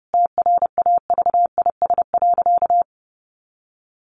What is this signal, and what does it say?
Morse code, heard as a keyed tone, about 30 wpm, 700 Hz. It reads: TRAVIS.
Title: A Sound Of Thunder 30wpm